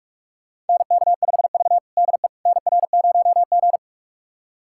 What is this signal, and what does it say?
Morse code, heard as a keyed tone, about 42 wpm, 700 Hz. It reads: NK5V DE NR0G
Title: USA Callsign DE Callsign - 4 Characters - 45wpm